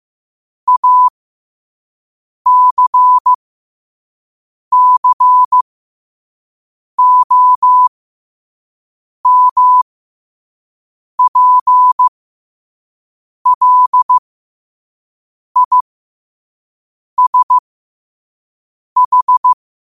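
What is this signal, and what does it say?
Morse code, heard as a keyed tone, about 15 wpm, spaced as for 5 wpm, 1000 Hz. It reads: ACCOMPLISH
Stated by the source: Morse code quote of the day sent at 5 WPM.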